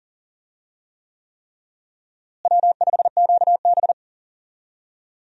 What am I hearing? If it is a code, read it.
Morse code, heard as a keyed tone, about 40 wpm, 700 Hz. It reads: W5QB
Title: USA Call Signs - 4 Characters - Repeated 3x - 40wpm